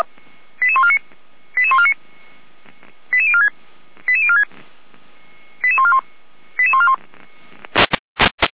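Five-tone selective call (ZVEI), heard as quick runs of five tones, tones of about 70 ms. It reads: 88149 88149 88046 88046 88141 88141
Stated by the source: ZVEI five-tone selective call received on SDR, from feuerwehr-orlamuende.org (Orlamünde fire brigade).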